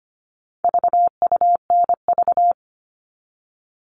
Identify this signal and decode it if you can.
Morse code, heard as a keyed tone, about 25 wpm, 700 Hz. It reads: VUNV
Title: All Cut Numbers - 4-Digit 25wpm